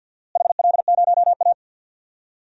Morse code, heard as a keyed tone, about 52 wpm, 700 Hz. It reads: HL0A
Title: DXCC-100 Call Signs - 4 Characters - Repeated 2x - 50wpm